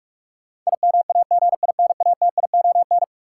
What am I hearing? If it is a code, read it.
Morse code, heard as a keyed tone, about 46 wpm, 700 Hz. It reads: IMAGINATION
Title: Sentences from Top 1300 Words - New Word Drill - 45wpm